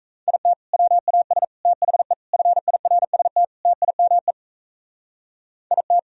Morse code, heard as a keyed tone, about 36 wpm, 700 Hz. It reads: IT WAS THE FIRST TIME IT